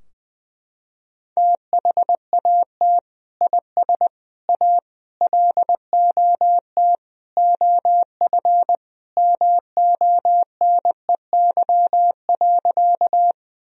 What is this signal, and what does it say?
Morse code, heard as a keyed tone, about 21 wpm, 700 Hz. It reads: THAT IS A LOT OF MONEY.
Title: Sentences from Top 1000 Words - Review Part 3 - 20wpm